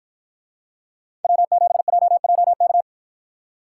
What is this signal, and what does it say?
Morse code, heard as a keyed tone, about 50 wpm, 700 Hz. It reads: W7JJK